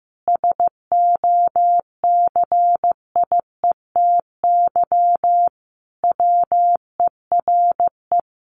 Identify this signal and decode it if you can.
Morse code, heard as a keyed tone, about 15 wpm, 700 Hz. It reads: SOCIETY WERE
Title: Sentences from Top 900 Words 15wpm